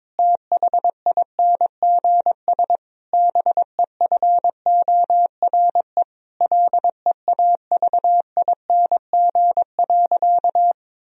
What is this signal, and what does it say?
Morse code, heard as a keyed tone, about 22 wpm, 700 Hz. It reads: THINGS BEFORE LEAVING.
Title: Sentences from Top 1000 Words - Review Part 4 - 22wpm